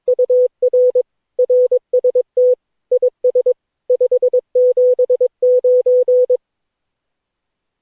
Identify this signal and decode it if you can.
Morse code, heard as a keyed tone, about 22 wpm, 500 Hz. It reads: UR RST IS 579
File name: cw / Session9 cw training